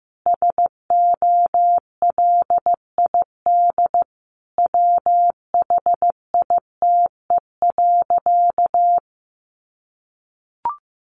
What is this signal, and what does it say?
Morse code, heard as a keyed tone, about 15 wpm, 700 Hz. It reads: SOLID WHITE.
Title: Sentences from Top 1500 Words 15wpm